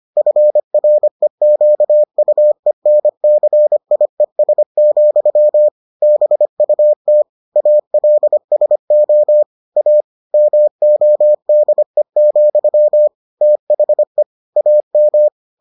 Morse code, heard as a keyed tone, about 25 wpm, 600 Hz. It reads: FREQUENCIES, BUT ALSO A MODE, THE AM